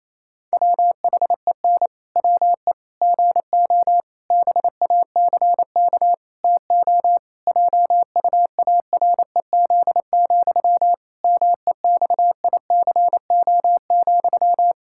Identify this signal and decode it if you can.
Morse code, heard as a keyed tone, about 28 wpm, 700 Hz. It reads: WHEN WE GO BACK TO JUAREZ, MEXICO,
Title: Pangram Challenge v2 - 28wpm